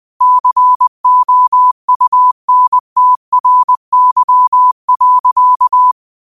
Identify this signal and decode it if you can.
Morse code, heard as a keyed tone, about 20 wpm, 1000 Hz. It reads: COUNTRY.
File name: qotdfiles / 20wpm